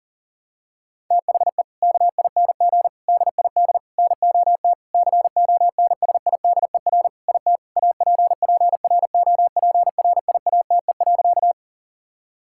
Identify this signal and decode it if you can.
Morse code, heard as a keyed tone, about 37 wpm, 700 Hz. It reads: THE KING DID NOT CONSIDER IT APPROPRIATE.
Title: Sentences from Top 1000 Words - Review Part 5 - 40wpm